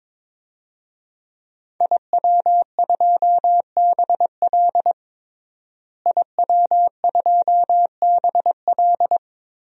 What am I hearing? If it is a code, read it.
Morse code, heard as a keyed tone, about 22 wpm, 700 Hz. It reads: IW2BL IW2BL
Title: DXCC-100 Call Signs - 5 Characters - Repeated 3x - 22wpm